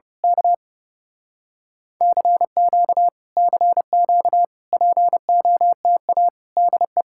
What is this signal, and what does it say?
Morse code, heard as a keyed tone, about 28 wpm, 700 Hz. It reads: K CQ CQ POTA DE
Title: Speed-Racing Common QSO Phrases Encore - 30wpm